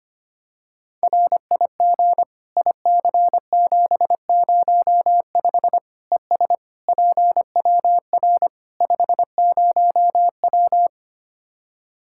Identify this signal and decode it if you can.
Morse code, heard as a keyed tone, about 25 wpm, 700 Hz. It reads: RIG IC705 ES PWR 50W
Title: Practice QSO - Exchange 2 - Info - 25wpm